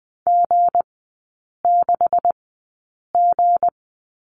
Morse code, heard as a keyed tone, about 20 wpm, spaced as for 8 wpm, 690 Hz. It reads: G6G
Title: 159 Lesson 26e - Letter g - Three Characters 20wpm